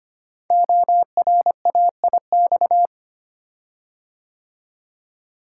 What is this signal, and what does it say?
Morse code, heard as a keyed tone, about 25 wpm, 700 Hz. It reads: ORAIX
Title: Code Groups - 5 Letters - 25wpm